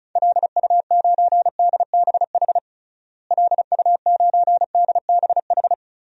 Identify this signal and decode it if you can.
Morse code, heard as a keyed tone, about 36 wpm, 700 Hz. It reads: LU9DBH LU9DBH
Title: DXCC-100 Call Signs - 6 Characters - Repeated 2x - 35wpm